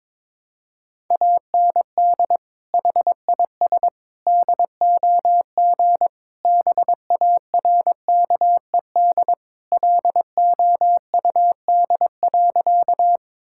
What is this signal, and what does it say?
Morse code, heard as a keyed tone, about 22 wpm, 700 Hz. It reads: AND HIS DOG BARKED LOUD.
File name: practice / 22wpm / Sentences with 2-Word Phrases - 22wpm